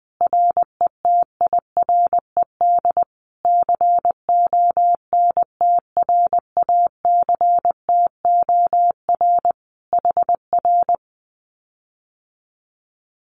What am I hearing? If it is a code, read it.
Morse code, heard as a keyed tone, about 20 wpm, 700 Hz. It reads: RETIRED CONTRACTOR HR
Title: Common QSO Phrases Encore 20wpm